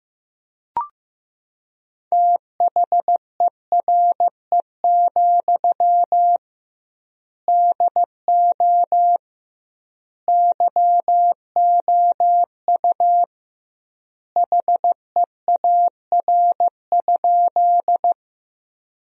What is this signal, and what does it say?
Morse code, heard as a keyed tone, about 15 wpm, 700 Hz. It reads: THERE, DO YOU HEAR?
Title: 2x Word-Spacing - Sentences from Top 300 Words - 15wpm